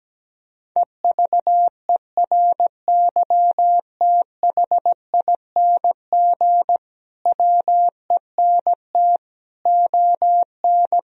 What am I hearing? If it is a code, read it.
Morse code, heard as a keyed tone, about 17 wpm, 700 Hz. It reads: EVERYTHING WENT ON
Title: Sentences from Top 1000 Words - Review Part 4 - 17wpm